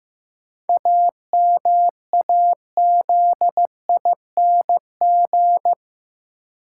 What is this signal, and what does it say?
Morse code, heard as a keyed tone, about 15 wpm, 700 Hz. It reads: AMAZING
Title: Sentences from Top 1500 Words - New Word Drill - 15wpm